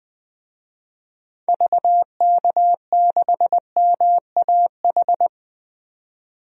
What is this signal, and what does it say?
Morse code, heard as a keyed tone, about 20 wpm, 700 Hz. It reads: VK6MAH
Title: DXCC-100 Call Signs - 6 Characters - 20wpm